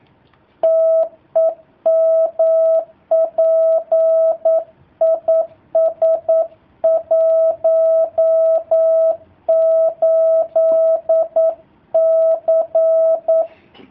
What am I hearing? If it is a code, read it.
Morse code, heard as a keyed tone, about 9 wpm, 650 Hz. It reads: TEMP IS 18C
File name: jh8cbh / 2019cw / sp09- w